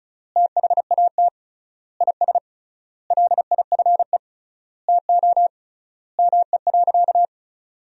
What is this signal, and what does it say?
Morse code, heard as a keyed tone, about 36 wpm, 700 Hz. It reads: THAT IS LIFE TO ME.
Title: 3x Word-Spacing - Sentences from Top 500 Words - 35wpm